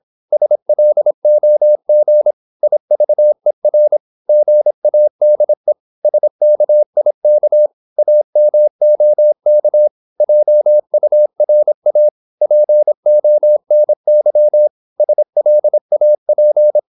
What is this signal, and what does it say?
Morse code, heard as a keyed tone, about 26 wpm, 600 Hz. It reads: SLOG IVER GADE SKIK AMOK JURA PONY SLAP
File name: Korte ord DK 26wpm